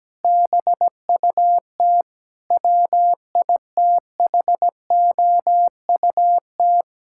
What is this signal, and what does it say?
Morse code, heard as a keyed tone, about 17 wpm, 700 Hz. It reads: BUT WITHOUT
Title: Sentences from Top 1000 Words - Review Part 6 - 17wpm